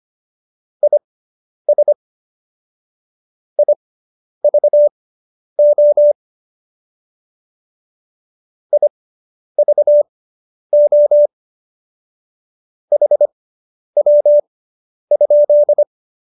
Morse code, heard as a keyed tone, about 25 wpm, spaced as for 10 wpm, 590 Hz. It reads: IS IVO IVO HW?